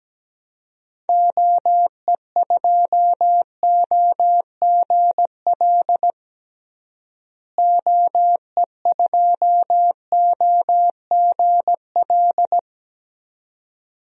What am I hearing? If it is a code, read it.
Morse code, heard as a keyed tone, about 17 wpm, 700 Hz. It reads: OE2OGL OE2OGL
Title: DXCC-100 Call Signs - 6 Characters - Repeated 2x - 17wpm